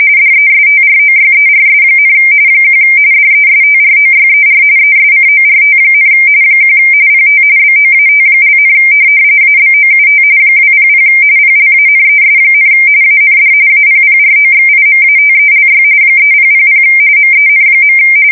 RTTY, about 45.45 baud. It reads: DESTROYERS, 1 DESTROYER ESCORT, 1 PT BOAT), ABOUT 200 PLANES, AND APPROXIMATELY 3000 MEN.
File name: fsk170 / USS-New-Jersey